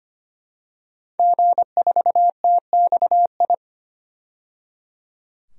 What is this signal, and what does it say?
Morse code, heard as a keyed tone, about 25 wpm, 700 Hz. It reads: G4TXI
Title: DXCC-100 Call Signs - 5 Characters - 25wpm